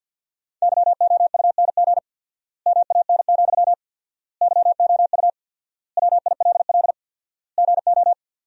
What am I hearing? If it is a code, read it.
Morse code, heard as a keyed tone, about 50 wpm, 700 Hz. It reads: YOUNG MAN, YOU WILL GO
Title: 4x Word-Spacing - Sentences from Top 400 Words - 50wpm